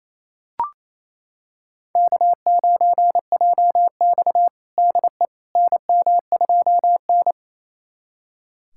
Morse code, heard as a keyed tone, about 28 wpm, 700 Hz. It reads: K9JX DE NM2N
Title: USA Callsign DE Callsign - 4 Characters - 28wpm